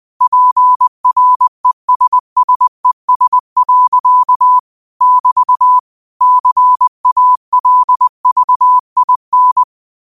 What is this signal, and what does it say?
Morse code, heard as a keyed tone, about 20 wpm, 1000 Hz. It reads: PRESSES. <BT> CALVIN